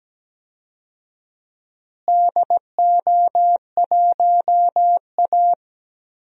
Morse code, heard as a keyed tone, about 17 wpm, 700 Hz. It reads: DO1A
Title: DXCC-100 Call Signs - 4 Characters - Repeated 3x - 17wpm